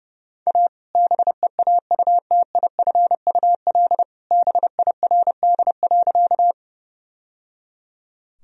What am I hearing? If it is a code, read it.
Morse code, heard as a keyed tone, about 32 wpm, 700 Hz. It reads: A BEAUTIFUL BIRD.
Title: Sentences from Top 1000 Words - Review Part 2 - 30wpm